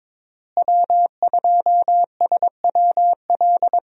Morse code, heard as a keyed tone, about 22 wpm, 700 Hz. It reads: W2SWL
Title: USA Call Signs - 5 Characters - Repeated 2x - 22wpm